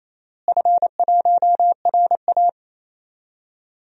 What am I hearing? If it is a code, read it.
Morse code, heard as a keyed tone, about 27 wpm, 700 Hz. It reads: F1RA
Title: DXCC-100 Call Signs - 4 Characters - Repeated 2x - 28wpm